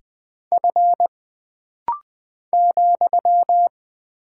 Morse code, heard as a keyed tone, about 20 wpm, 700 Hz. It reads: F ,
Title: Single Character (Letter:Number:Punctuation:Prosign) - Rapid-Fire - 20wpm